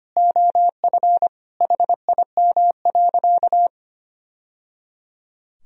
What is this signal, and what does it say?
Morse code, heard as a keyed tone, about 25 wpm, 700 Hz. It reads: OF HIM.
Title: Sentences from Top 1000 Words - Review Part 3 - 25wpm